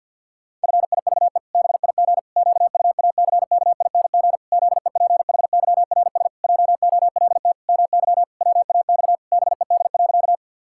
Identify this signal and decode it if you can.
Morse code, heard as a keyed tone, about 52 wpm, 700 Hz. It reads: FIVE BIG QUACKING ZEPHYRS JOLT MY WAX BED.
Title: Pangram Challenge v2 - 50wpm